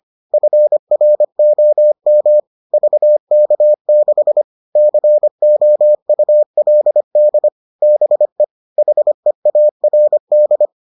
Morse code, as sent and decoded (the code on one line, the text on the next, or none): ..-. .-. --- -- / ...- -.- -.... / -.-. --- ..- .-.. -.. / -... / .... .- .-. -..
FROM VK6 COULD BE HEARD